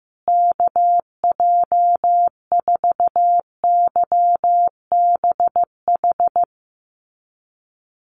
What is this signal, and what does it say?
Morse code, heard as a keyed tone, about 15 wpm, 700 Hz. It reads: KJ4YBH
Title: USA Call Signs Repeated 2x 15wpm